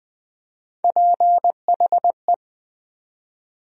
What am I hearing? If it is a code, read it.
Morse code, heard as a keyed tone, about 20 wpm, 700 Hz. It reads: PHE
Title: USA Call Sign Postfixes - 3 Letters 20wpm